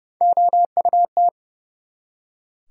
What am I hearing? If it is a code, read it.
Morse code, heard as a keyed tone, about 43 wpm, spaced as for 37 wpm, 700 Hz. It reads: OUT
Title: Speed-Racing Top 100 Words 30wpm